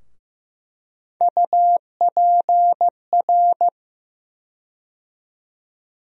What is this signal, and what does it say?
Morse code, heard as a keyed tone, about 15 wpm, 700 Hz. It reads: UPR
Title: USA Call Sign Postfixes - 3 Letters 15wpm